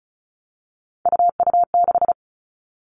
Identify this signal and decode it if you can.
Morse code, heard as a keyed tone, about 34 wpm, 700 Hz. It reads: UU6
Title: All Cut Numbers - 3-Digit 35wpm